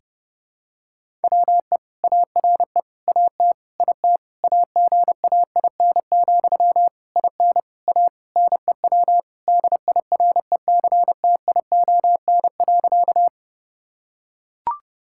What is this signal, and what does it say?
Morse code, heard as a keyed tone, about 30 wpm, 700 Hz. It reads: WE ARE AT IT AGAIN, IN A NEW DIRECTION.
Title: Sentences from Top 1000 Words - Review Part 5 - 30wpm